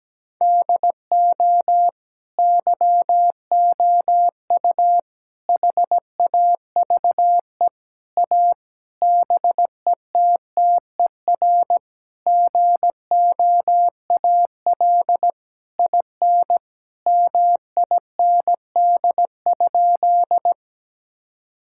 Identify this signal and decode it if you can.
Morse code, heard as a keyed tone, about 17 wpm, 700 Hz. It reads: DO YOU HAVE A BETTER GOAL IN MIND?
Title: Sentences from Top 1000 Words - Review Part 5 - 17wpm